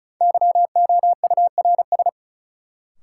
Morse code, heard as a keyed tone, about 35 wpm, 700 Hz. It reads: YOURS
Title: Sentences from Top 1500 Words - New Word Drill Alt - 35wpm